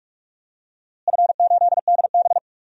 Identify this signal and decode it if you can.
Morse code, heard as a keyed tone, about 46 wpm, 700 Hz. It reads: F8DB
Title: DXCC-100 Call Signs - 4 Characters - Repeated 3x - 45wpm